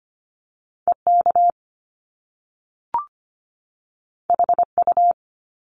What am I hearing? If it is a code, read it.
Morse code, heard as a keyed tone, about 25 wpm, 700 Hz. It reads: EK HU
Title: Top Two Letter Combinations in Words 25wpm